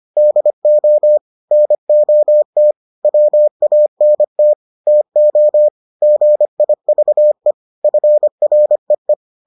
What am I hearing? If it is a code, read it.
Morse code, heard as a keyed tone, about 25 wpm, 600 Hz. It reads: DO NOT WANT TO GIVE FREE